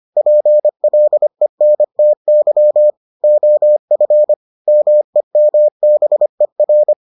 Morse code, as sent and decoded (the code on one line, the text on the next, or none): .--. .-.. -. - -.-- / --- ..-. / -- . -- -... .-.
PLENTY OF MEMBER